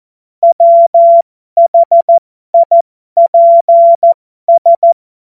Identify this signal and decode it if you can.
Morse code, heard as a keyed tone, about 14 wpm, spaced as for 12 wpm, 680 Hz. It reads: WHIPS
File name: Morsekurs / Witze